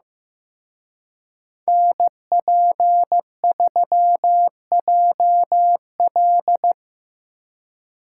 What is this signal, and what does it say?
Morse code, heard as a keyed tone, about 18 wpm, 700 Hz. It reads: NP3JL
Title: Speed Racing - USA 2x2 Call Signs 15wpm